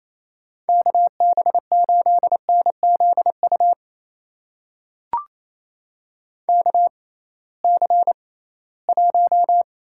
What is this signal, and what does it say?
Morse code, heard as a keyed tone, about 28 wpm, spaced as for 18 wpm, 700 Hz. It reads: KB8NZU KC1
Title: USA Call Signs - 6 Characters - 6x Character Spacing - 28wpm